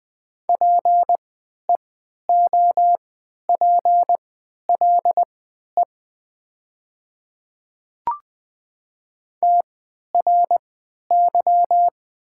Morse code, heard as a keyed tone, about 20 wpm, spaced as for 11 wpm, 700 Hz. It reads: PEOPLE TRY
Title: 3x Character Spacing - Top 200 Words - 20wpm